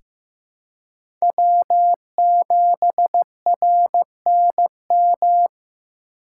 Morse code, heard as a keyed tone, about 15 wpm, 700 Hz. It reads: W7RNM
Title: USA Call Signs - 3x Character Spacing - 15wpm